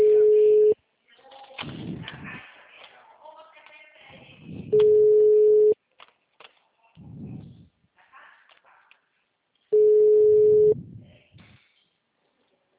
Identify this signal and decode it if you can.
Ringback tone (Europe, 425 Hz), with speech.